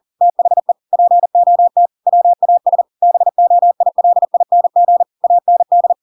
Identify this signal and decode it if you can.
Morse code, heard as a keyed tone, about 40 wpm, 700 Hz. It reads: THE POT WAS BOILING AND
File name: Pinocchio / Pinocchio ch5 40 40 ik1hge